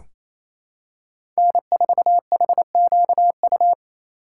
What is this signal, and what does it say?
Morse code, heard as a keyed tone, about 28 wpm, 700 Hz. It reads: N4HQU
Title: USA Call Signs - 5 Characters - 8x Character Spacing - 28wpm